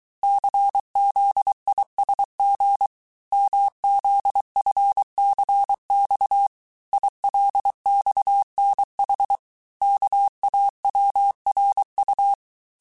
Morse code, heard as a keyed tone, about 23 wpm, 800 Hz. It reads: CZISG MZFCX ILXNH KAWRU
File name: cours-du-mercredi / audios / cours-855-texte-chiffre-vitesse-22-mpm